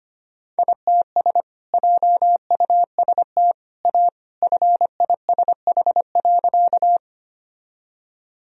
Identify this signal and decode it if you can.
Morse code, heard as a keyed tone, about 25 wpm, 700 Hz. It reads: ITS JUST A FISH.
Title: Sentences from Top 1000 Words - Review Part 1 - 25wpm